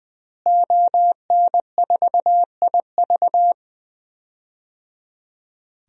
Morse code, heard as a keyed tone, about 20 wpm, 700 Hz. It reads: ON4IV
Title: DXCC-100 Call Signs - 5 Characters - Repeated 3x - 20wpm